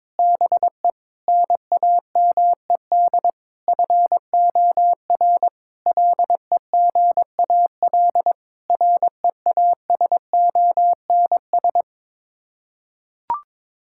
Morse code, heard as a keyed tone, about 22 wpm, 700 Hz. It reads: BE NAMED FOR LEGAL REASONS
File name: practice / 22wpm / Sets of 8 Words - 22wpm